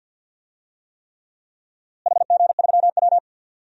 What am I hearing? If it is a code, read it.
Morse code, heard as a keyed tone, about 52 wpm, 700 Hz. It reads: HG3W
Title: DXCC-100 Call Signs - 4 Characters - Repeated 3x - 50wpm